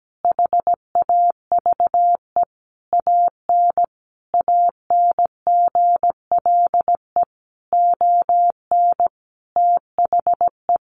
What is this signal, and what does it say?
Morse code, heard as a keyed tone, about 17 wpm, 700 Hz. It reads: HAVE AN ANGLE ON THE
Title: Sentences from Top 1500 Words 17wpm